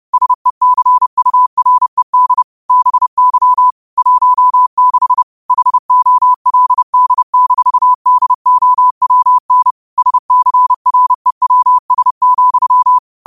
Morse code, heard as a keyed tone, about 30 wpm, 1000 Hz. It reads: SECURED BY 16 HOLD-DOWN SCREWS,